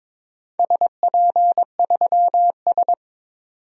Morse code, heard as a keyed tone, about 22 wpm, 700 Hz. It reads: SP3S
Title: DXCC-100 Call Signs - 4 Characters - 22wpm